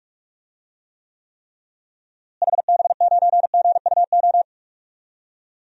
Morse code, heard as a keyed tone, about 46 wpm, 700 Hz. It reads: HB9GUO
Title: DXCC-100 Call Signs - 6 Characters - Repeated 3x - 45wpm